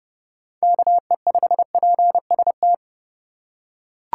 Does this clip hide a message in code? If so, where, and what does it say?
Yes: Morse code from 0.6 to 2.7 s, heard as a keyed tone, about 30 wpm, 700 Hz. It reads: KE5PST